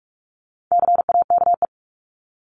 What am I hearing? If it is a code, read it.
Morse code, heard as a keyed tone, about 44 wpm, 700 Hz. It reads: CAKE
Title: Top 1501-1750 Words 45wpm